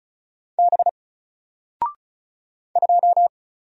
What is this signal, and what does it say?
Morse code, heard as a keyed tone, about 35 wpm, 700 Hz. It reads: B 2
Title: Single Character (Letter:Number:Punctuation:Prosign) 35wpm